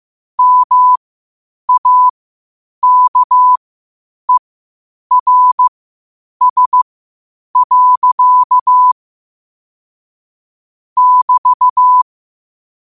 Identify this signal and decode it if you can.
Morse code, heard as a keyed tone, about 15 wpm, spaced as for 8 wpm, 1000 Hz. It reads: MAKERS. <BT>
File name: qotdfiles / 8wpm